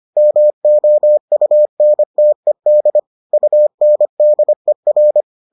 Morse code, heard as a keyed tone, about 25 wpm, 600 Hz. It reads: MOUNTED UNDER